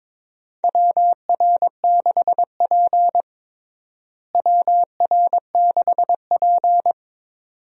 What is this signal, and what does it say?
Morse code, heard as a keyed tone, about 22 wpm, 700 Hz. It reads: WR6P WR6P
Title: USA Call Signs - 4 Characters - Repeated 2x - 22wpm